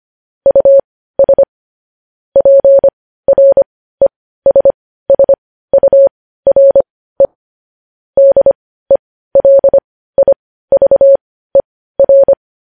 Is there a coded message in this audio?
Yes: Morse code from 0.5 to 12.3 s, heard as a keyed tone, about 25 wpm, spaced as for 15 wpm, 550 Hz. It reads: US PRESSURE DELIVER